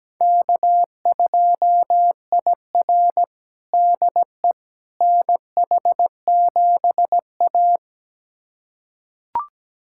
Morse code, heard as a keyed tone, about 17 wpm, 700 Hz. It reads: K2IR DE NH7A
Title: USA Callsign DE Callsign - 4 Characters - 17wpm